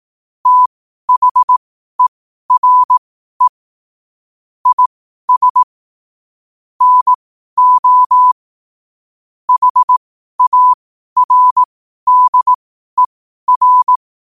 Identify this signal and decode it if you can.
Morse code, heard as a keyed tone, about 18 wpm, spaced as for 12 wpm, 1000 Hz. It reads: THERE IS NO HARDER
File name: qotdfiles / 12wpm